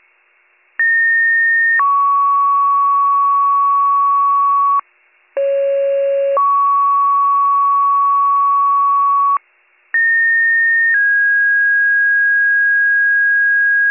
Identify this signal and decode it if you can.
Two-tone page, heard as a one-second tone, then a three-second tone, about 1850 then 1120 Hz, about 550 then 1120 Hz, about 1850 then 1650 Hz, with noise. It is heard from 0.8 to 13.9 s.